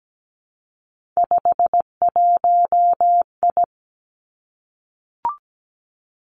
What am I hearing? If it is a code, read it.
Morse code, heard as a keyed tone, about 17 wpm, 700 Hz. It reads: 51I
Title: Three Challenging Letter-Number Combinations 17wpm